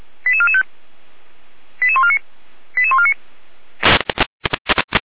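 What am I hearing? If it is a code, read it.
Five-tone selective call (ZVEI), heard as quick runs of five tones, tones of about 70 ms. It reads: 88445 88149 88149
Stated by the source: ZVEI five-tone selective call received on SDR.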